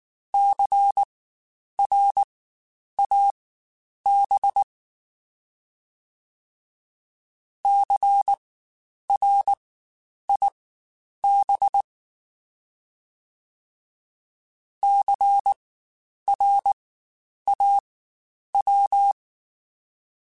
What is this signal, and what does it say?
Morse code, heard as a keyed tone, about 19 wpm, spaced as for 9 wpm, 800 Hz. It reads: CRAB CRIB CRAW
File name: Spaced Alphas / Spaced Alphas mp3s / Cs sp @18